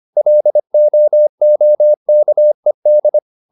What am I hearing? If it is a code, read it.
Morse code, heard as a keyed tone, about 25 wpm, 600 Hz. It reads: LOOKED